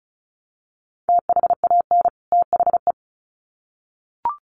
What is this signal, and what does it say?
Morse code, heard as a keyed tone, about 34 wpm, 700 Hz. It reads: THAN THE
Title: Sets of 2 Words 35wpm